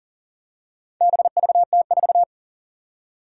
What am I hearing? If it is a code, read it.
Morse code, heard as a keyed tone, about 40 wpm, 700 Hz. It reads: BVT4
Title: Four Challenging Letter-Number Combinations 40wpm